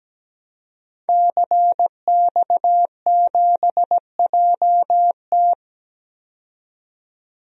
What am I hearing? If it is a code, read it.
Morse code, heard as a keyed tone, about 17 wpm, 700 Hz. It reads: CX7JT
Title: DXCC-100 Call Signs - 5 Characters - 17wpm